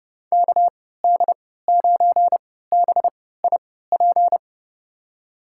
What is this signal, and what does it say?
Morse code, heard as a keyed tone, about 32 wpm, spaced as for 17 wpm, 700 Hz. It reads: KD9BIP
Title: USA Call Signs - 6 Characters - 3x Character Spacing - 30wpm